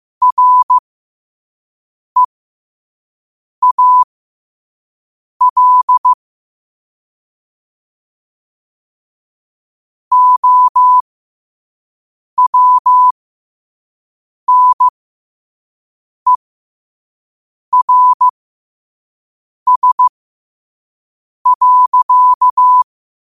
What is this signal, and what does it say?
Morse code, heard as a keyed tone, about 15 wpm, spaced as for 5 wpm, 1000 Hz. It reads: REAL OWNERS.
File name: qotdfiles / 5wpm